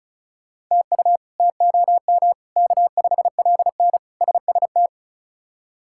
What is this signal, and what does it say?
Morse code, heard as a keyed tone, about 36 wpm, 700 Hz. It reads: TU TOM K5LN SST
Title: K1USN SST QSO Practice - 35wpm